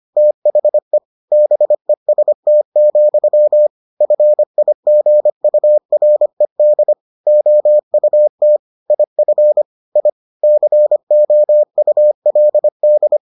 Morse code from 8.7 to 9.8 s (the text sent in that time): IF